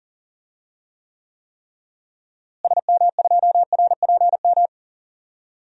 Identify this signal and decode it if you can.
Morse code, heard as a keyed tone, about 40 wpm, 700 Hz. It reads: SM2RPM
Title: DXCC-100 Call Signs - 6 Characters - Repeated 3x - 40wpm